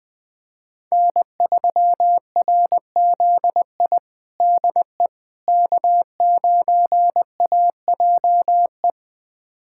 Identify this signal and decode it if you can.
Morse code, heard as a keyed tone, about 21 wpm, 700 Hz. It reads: N3RZI DE K9AJE
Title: USA Callsign DE Callsign - 5 Characters - 20wpm